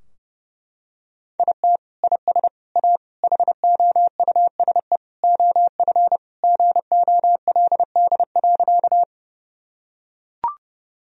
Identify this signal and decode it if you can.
Morse code, heard as a keyed tone, about 29 wpm, 700 Hz. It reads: IT IS A HOUSE OF GOLD.
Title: Sentences from Top 1000 Words - Review Part 3 - 30wpm